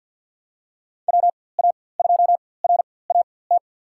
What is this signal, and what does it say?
Morse code, heard as a keyed tone, about 51 wpm, spaced as for 23 wpm, 700 Hz. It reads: WA2RAT
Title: USA Call Signs - 6 Characters - 4x Character Spacing - 50wpm